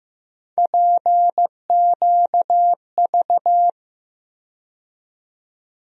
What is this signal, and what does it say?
Morse code, heard as a keyed tone, about 15 wpm, 700 Hz. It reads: PQV
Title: USA Call Sign Postfixes - 3 Letters 15wpm